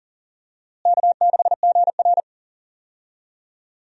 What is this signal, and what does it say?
Morse code, heard as a keyed tone, about 39 wpm, 700 Hz. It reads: K6GR
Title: USA Call Signs - 4 Characters - Repeated 3x - 40wpm